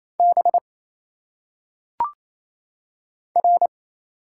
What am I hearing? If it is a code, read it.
Morse code, heard as a keyed tone, about 27 wpm, 700 Hz. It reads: B R
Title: Single Letters - Phonetic Pronunciation 28wpm